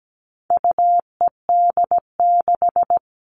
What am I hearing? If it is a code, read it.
Morse code, heard as a keyed tone, about 17 wpm, 700 Hz. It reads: UED6
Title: All Cut Numbers - 4-Digit 17wpm